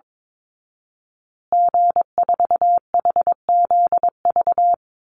Morse code, heard as a keyed tone, about 22 wpm, 700 Hz. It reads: G4HZV
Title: Call Signs Repeated 3x - 22wpm